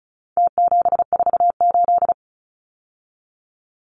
Morse code, heard as a keyed tone, about 34 wpm, 700 Hz. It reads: T748
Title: Numbers - Contest Style - 4-Digit 35wpm